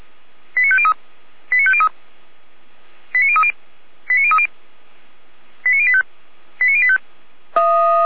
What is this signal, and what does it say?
Five-tone selective call (ZVEI), heard as quick runs of five tones, tones of about 70 ms. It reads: 89503 89503 89030 89030 89085 89085
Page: /455/SDR/ZVEI/89115/2021.08.03/